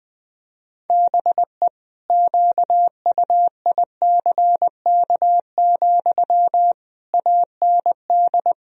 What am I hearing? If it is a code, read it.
Morse code, heard as a keyed tone, about 21 wpm, 700 Hz. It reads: BE QUICK, AND
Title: Sentences from Top 1000 Words - Review Part 3 - 20wpm